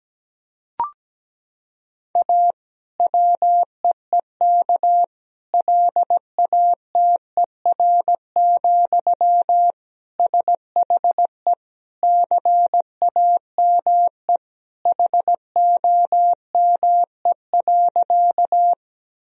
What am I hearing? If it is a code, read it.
Morse code, heard as a keyed tone, about 17 wpm, 700 Hz. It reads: A WEEK LATER, SHE CAME HOME.
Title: Sentences from Top 1000 Words - Review Part 4 - 17wpm